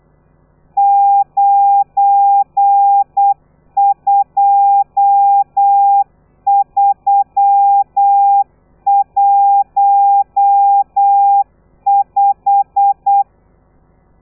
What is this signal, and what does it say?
Morse code, heard as a keyed tone, about 8 wpm, 800 Hz. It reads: 92315